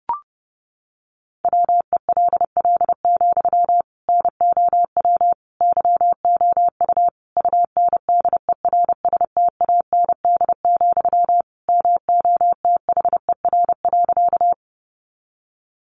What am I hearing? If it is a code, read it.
Morse code, heard as a keyed tone, about 30 wpm, 700 Hz. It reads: WELL, NOW YOU UNDERSTAND, MOTHER.
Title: Sentences from Top 600 Words 30wpm